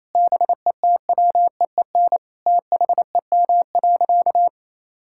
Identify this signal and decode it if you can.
Morse code, heard as a keyed tone, about 27 wpm, 700 Hz. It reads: BETWEEN THEM.
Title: Sentences from Top 1000 Words - Review Part 6 - 28wpm